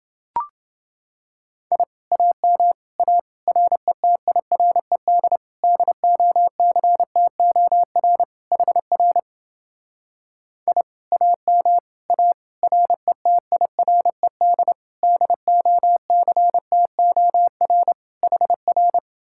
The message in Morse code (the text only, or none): I AM A RETIRED DOCTOR HR I AM A RETIRED DOCTOR HR